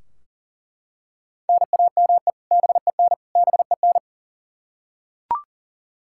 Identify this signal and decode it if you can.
Morse code, heard as a keyed tone, about 40 wpm, 700 Hz. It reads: NAME BEN BEN
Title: K1USN SST QSO Practice 40wpm